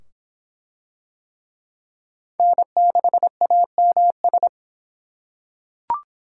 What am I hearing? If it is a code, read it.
Morse code, heard as a keyed tone, about 25 wpm, 700 Hz. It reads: N6AMS